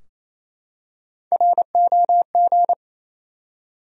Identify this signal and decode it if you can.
Morse code, heard as a keyed tone, about 28 wpm, 700 Hz. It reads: ROG
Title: USA Common Names - Max 3 Letters - 28wpm